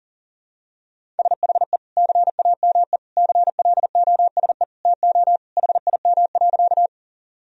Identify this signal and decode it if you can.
Morse code, heard as a keyed tone, about 37 wpm, 700 Hz. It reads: SHE CAME CLOSE TO HIM.
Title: Sentences from Top 1000 Words - Review Part 1 - 40wpm